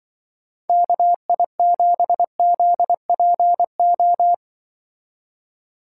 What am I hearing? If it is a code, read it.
Morse code, heard as a keyed tone, about 24 wpm, 700 Hz. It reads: KI7ZPO
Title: Speed Racing - USA Call Signs 20wpm